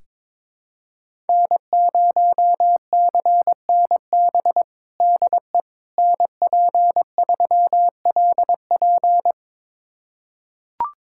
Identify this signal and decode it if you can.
Morse code, heard as a keyed tone, about 22 wpm, 700 Hz. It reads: N0CNB DE NP3LP
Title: USA Callsign DE Callsign - 5 Characters - 22wpm